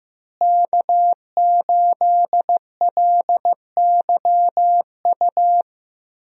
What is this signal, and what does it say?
Morse code, heard as a keyed tone, about 15 wpm, 700 Hz. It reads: K8LYU
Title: USA Call Signs - 5 Characters - Repeated 3x - 15wpm